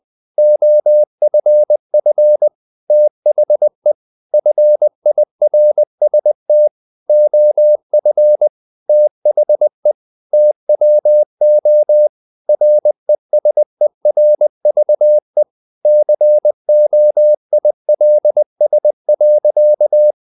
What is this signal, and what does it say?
Morse code, heard as a keyed tone, about 20 wpm, 600 Hz. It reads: OFF THE FIRST OF THE TWO RESERVE COILS.
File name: the old man and the sea / the old man and the sea part 1 20wpm